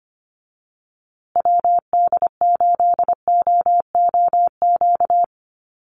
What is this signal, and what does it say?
Morse code, heard as a keyed tone, about 25 wpm, 700 Hz. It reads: WD8OOQ
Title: USA Call Signs Repeated 2x 25wpm